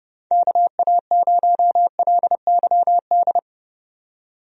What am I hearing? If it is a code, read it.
Morse code, heard as a keyed tone, about 30 wpm, 700 Hz. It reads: KA0LYD
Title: USA Call Signs - 6 Characters - 30wpm